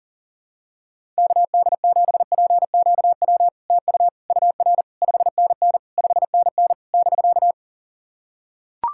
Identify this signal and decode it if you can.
Morse code, heard as a keyed tone, about 42 wpm, 700 Hz. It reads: KD7PQW TU UR 5NN 5NN <BK>
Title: SOTA QSO Practice 40wpm